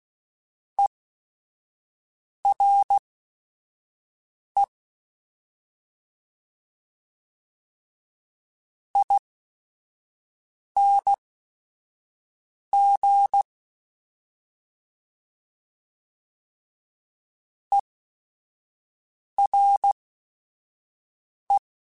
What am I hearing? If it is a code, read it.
Morse code, heard as a keyed tone, about 16 wpm, spaced as for 5 wpm, 800 Hz. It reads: ERE ING ERE